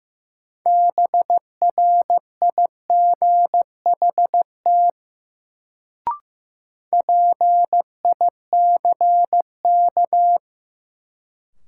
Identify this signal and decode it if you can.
Morse code, heard as a keyed tone, about 15 wpm, 700 Hz. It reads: BRIGHT PICK
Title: Sentences from Top 1000 Words - New Word Drill - 15wpm